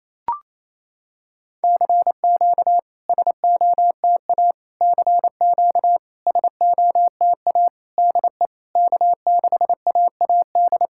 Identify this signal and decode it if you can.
Morse code, heard as a keyed tone, about 28 wpm, 700 Hz. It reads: CQ SOTA CQ SOTA DE K6AAD
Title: SOTA QSO Practice 28wpm